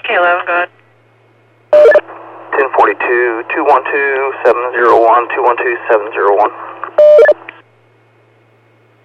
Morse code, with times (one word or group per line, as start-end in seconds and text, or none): none